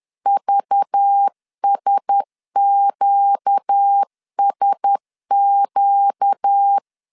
Morse code, heard as a keyed tone, about 10 wpm, 800 Hz. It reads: VSQSQ
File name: CW / DET / 1 6 1500C S B 10wpm 4137